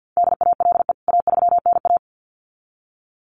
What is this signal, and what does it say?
Morse code, heard as a keyed tone, about 49 wpm, 700 Hz. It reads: DALE A3NA
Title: CWT Exchange 50wpm